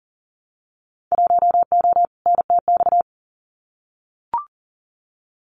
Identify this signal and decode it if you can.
Morse code, heard as a keyed tone, about 38 wpm, 700 Hz. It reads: 1O NTX
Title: Winter Field Day Exchanges 40wpm